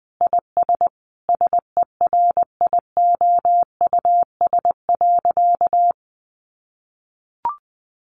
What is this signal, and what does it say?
Morse code, heard as a keyed tone, about 20 wpm, 700 Hz. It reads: IS SERIOUS.
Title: Sentences from Top 1100 Words 20wpm